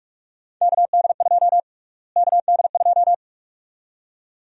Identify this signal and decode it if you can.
Morse code, heard as a keyed tone, about 46 wpm, 700 Hz. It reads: KD2 KD2
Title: USA Call Sign Prefixes - Repeated 2x - 45wpm